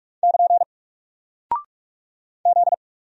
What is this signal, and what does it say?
Morse code, heard as a keyed tone, about 44 wpm, 700 Hz. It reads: <KN> Z
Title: Single Character (Letter:Number:Punctuation:Prosign) 45wpm